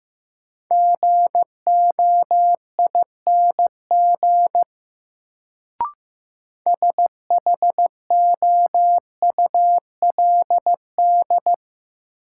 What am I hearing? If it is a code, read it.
Morse code, heard as a keyed tone, about 15 wpm, 700 Hz. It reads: GOING SHOULD
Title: Sets of 2 Words - New Word Drill - 15wpm